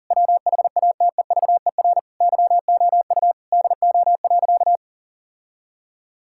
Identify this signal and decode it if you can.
Morse code, heard as a keyed tone, about 40 wpm, 700 Hz. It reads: WHATEVER YOU DO.
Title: Sentences from Top 1600 Words 40wpm